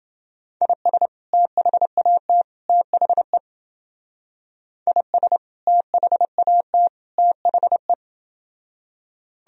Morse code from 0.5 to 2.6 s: IS THAT

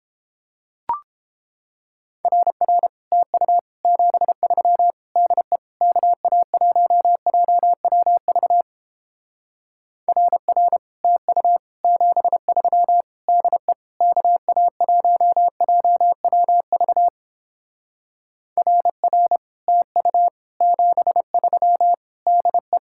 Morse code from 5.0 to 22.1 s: DE KA1JWV RR TU 73 DE KA1JWV RR TU 73

